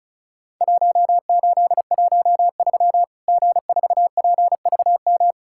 Morse code, heard as a keyed tone, about 36 wpm, 700 Hz. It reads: 1813 G4PVM
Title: Worked All Europe DX Contest - QTC Exchange - 35wpm